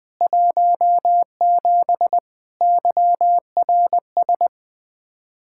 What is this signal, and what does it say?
Morse code, heard as a keyed tone, about 21 wpm, 700 Hz. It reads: 17 YRS
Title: Speed-Racing Common QSO Phrases - 20wpm